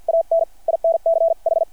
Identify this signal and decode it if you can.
Morse code, heard as a keyed tone, about 50 wpm, spaced as for 43 wpm, 660 Hz. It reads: AN INCH
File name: archived / CRSnet